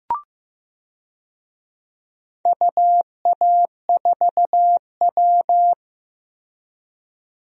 Morse code, heard as a keyed tone, about 16 wpm, 700 Hz. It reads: UA4W
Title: DXCC-100 Call Signs - 4 Characters - Repeated 3x - 15wpm